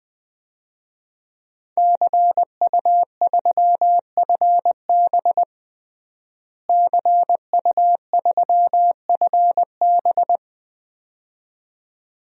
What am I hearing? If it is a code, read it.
Morse code, heard as a keyed tone, about 20 wpm, 700 Hz. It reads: CU3FB CU3FB
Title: DXCC-100 Call Signs - 5 Characters - Repeated 2x - 20wpm